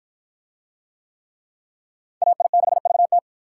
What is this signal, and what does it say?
Morse code, heard as a keyed tone, about 46 wpm, 700 Hz. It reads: AI6VT